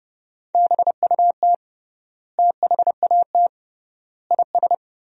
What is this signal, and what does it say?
Morse code, heard as a keyed tone, about 30 wpm, 700 Hz. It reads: BUT THAT IS
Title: 3x Word-Spacing - Sentences from Top 500 Words - 30wpm